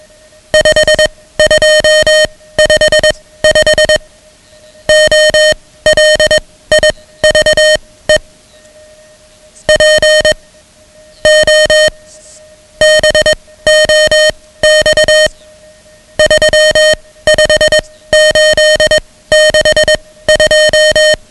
Morse code, heard as a keyed tone, about 21 wpm, spaced as for 15 wpm, 620 Hz, with noise. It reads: 5255 OLIVE P O BOX 35862
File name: ham radio / CodeQuick / Code Quick 10